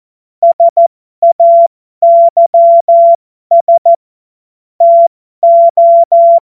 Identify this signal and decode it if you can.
Morse code, heard as a keyed tone, about 14 wpm, spaced as for 12 wpm, 680 Hz. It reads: SAYS TO
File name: Morsekurs / Witze